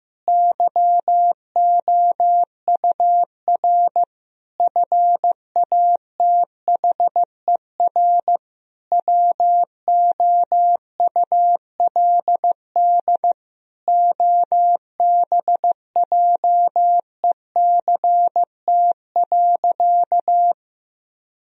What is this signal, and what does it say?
Morse code, heard as a keyed tone, about 16 wpm, 700 Hz. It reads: YOUR FATHER WOULD OBJECT.
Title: Sentences from Top 1000 Words - Review Part 3 - 15wpm